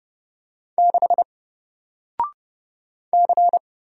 Morse code, heard as a keyed tone, about 30 wpm, 700 Hz. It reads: 6 C
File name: Single-Letter-Number-30wpm